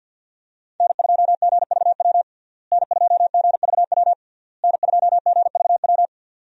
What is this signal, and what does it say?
Morse code, heard as a keyed tone, about 52 wpm, 700 Hz. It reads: N2GVW N2GVW N2GVW
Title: USA Call Signs - 5 Characters - Repeated 3x - 50wpm